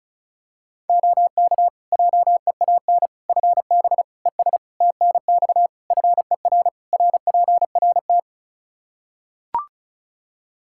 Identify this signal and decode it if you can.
Morse code, heard as a keyed tone, about 34 wpm, 700 Hz. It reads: OK JEAN FB ES TNX FER RPRT
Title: Practice QSO - Exchange 2 - Info - 35wpm